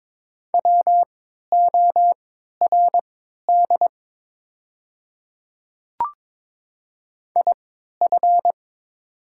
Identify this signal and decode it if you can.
Morse code, heard as a keyed tone, about 22 wpm, spaced as for 13 wpm, 700 Hz. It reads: WORD IF
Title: 3x Character Spacing - Top 100 Words - 22wpm